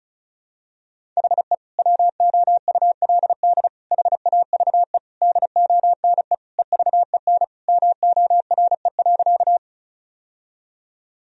Morse code, heard as a keyed tone, about 36 wpm, 700 Hz. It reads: HE WOULD HAVE DONE EVEN MORE.
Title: Sentences from Top 1000 Words - Review Part 1 - 35wpm